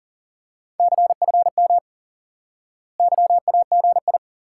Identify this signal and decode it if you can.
Morse code, heard as a keyed tone, about 38 wpm, 700 Hz. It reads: CFM YAGI
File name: IQE Course 3 - Lesson 89 - YAGI - Two Elements - 40wpm